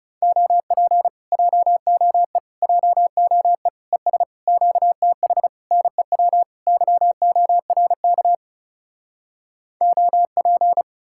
Morse code, heard as a keyed tone, about 40 wpm, 700 Hz. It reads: OP JOE JOE ES QTH NEW YORK OP